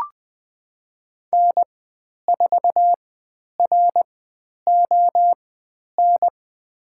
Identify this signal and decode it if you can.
Morse code, heard as a keyed tone, about 20 wpm, spaced as for 10 wpm, 700 Hz. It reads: N4RON
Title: 095 Lesson 16g - Letter c - Call Signs 20wpm